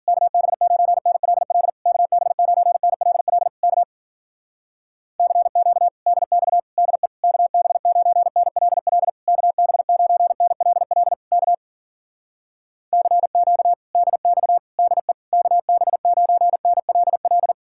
Morse code, heard as a keyed tone, about 52 wpm, 700 Hz. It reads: KB9NLL KB9NLL K CQ DX DE KB9NLL KB9NLL K CQ DX DE KB9NLL